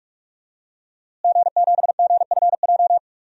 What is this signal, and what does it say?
Morse code, heard as a keyed tone, about 46 wpm, 700 Hz. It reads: G7GFJ
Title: DXCC-100 Call Signs - 5 Characters - Repeated 3x - 45wpm